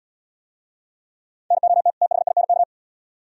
Morse code, heard as a keyed tone, about 74 wpm, 700 Hz. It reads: NOT THINK